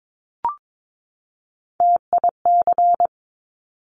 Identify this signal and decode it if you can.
Morse code, heard as a keyed tone, about 22 wpm, 700 Hz. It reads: TIC
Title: Top Three Letter Combinations in Words 22wpm